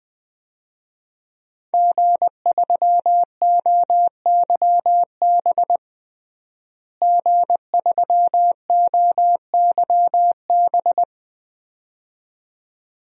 Morse code, heard as a keyed tone, about 20 wpm, 700 Hz. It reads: G3OYB G3OYB
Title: DXCC-100 Call Signs - 5 Characters - Repeated 2x - 20wpm